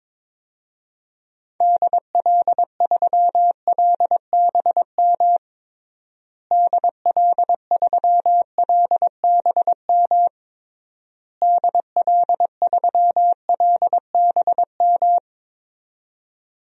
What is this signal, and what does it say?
Morse code, heard as a keyed tone, about 22 wpm, 700 Hz. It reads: DL3LBM DL3LBM DL3LBM
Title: DXCC-100 Call Signs - 6 Characters - Repeated 3x - 22wpm